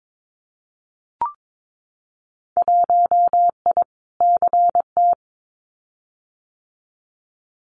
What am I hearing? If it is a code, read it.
Morse code, heard as a keyed tone, about 22 wpm, 700 Hz. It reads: 1I CT
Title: Winter Field Day Exchanges 22wpm